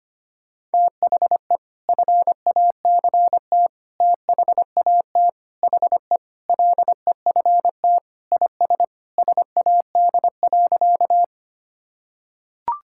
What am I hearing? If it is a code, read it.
Morse code, heard as a keyed tone, about 25 wpm, 700 Hz. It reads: THE FACT THAT HE LEFT IS SAD.
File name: practice / 25wpm / Sentences with 2-Word Phrases - 25wpm